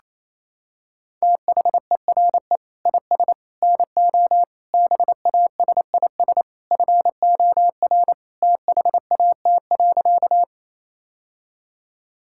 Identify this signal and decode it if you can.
Morse code, heard as a keyed tone, about 28 wpm, 700 Hz. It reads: THERE IS NO BASIS FOR THAT.
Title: Sentences from Top 1000 Words - Review Part 5 - 28wpm